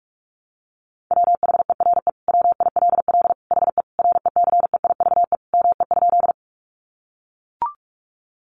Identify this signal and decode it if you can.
Morse code, heard as a keyed tone, about 44 wpm, 700 Hz. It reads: WHERE WILL HE RECEIVE ME?
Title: Sentences from Top 1000 Words 45wpm